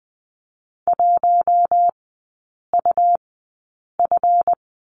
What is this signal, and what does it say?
Morse code, heard as a keyed tone, about 20 wpm, spaced as for 8 wpm, 700 Hz. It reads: 1UF
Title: 166 Lesson 27e - Number 7 - Three Characters 20wpm